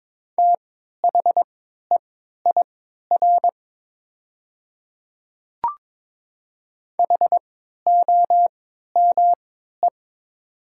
Morse code, heard as a keyed tone, about 22 wpm, spaced as for 12 wpm, 700 Hz. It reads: THEIR HOME
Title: 3x Character Spacing - Top 300 Words - 22wpm